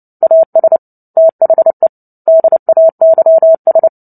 Morse code, heard as a keyed tone, about 30 wpm, 650 Hz. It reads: AS THE DAYS
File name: Book Two / 03-Chapter 3 - 29 WPM